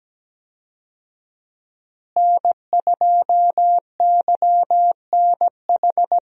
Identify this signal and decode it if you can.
Morse code, heard as a keyed tone, about 17 wpm, 700 Hz. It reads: N2YNH
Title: USA Call Signs - 5 Characters - Repeated 3x - 17wpm